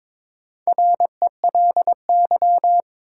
Morse code, heard as a keyed tone, about 21 wpm, 700 Hz. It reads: RELY
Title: Sentences from Top 1200 Words - New Word Drill Alt - 22wpm